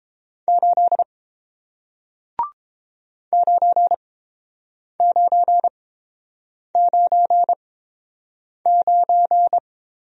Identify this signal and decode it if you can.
Morse code, heard as a keyed tone, about 31 wpm, spaced as for 7 wpm, 700 Hz. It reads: 8 9999